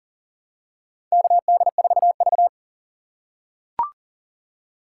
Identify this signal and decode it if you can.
Morse code, heard as a keyed tone, about 43 wpm, 700 Hz. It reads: KD4V
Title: USA Call Signs - 4 Characters - 6x Character Spacing - 40wpm